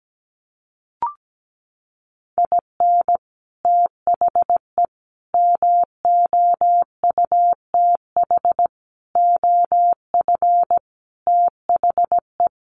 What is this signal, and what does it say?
Morse code, heard as a keyed tone, about 17 wpm, 700 Hz. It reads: IN THE MOUTH OF THE
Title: Sentences from Top 1500 Words 17wpm